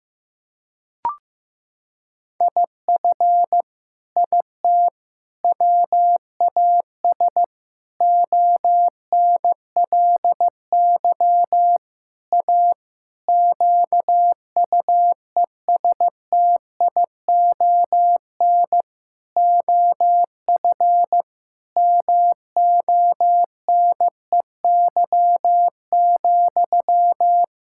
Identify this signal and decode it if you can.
Morse code, heard as a keyed tone, about 15 wpm, 700 Hz. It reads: IF IT WAS ONLY A QUESTION OF MONEY,
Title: Sentences from Top 1600 Words 15wpm